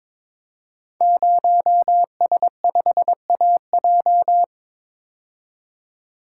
Morse code, heard as a keyed tone, about 22 wpm, 700 Hz. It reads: 0S5AJ
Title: Code Groups - 5 Letter:Numbers - 22wpm